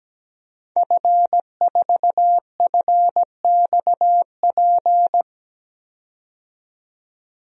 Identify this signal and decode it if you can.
Morse code, heard as a keyed tone, about 17 wpm, 700 Hz. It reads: F4FXP
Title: DXCC-100 Call Signs - 5 Characters - 17wpm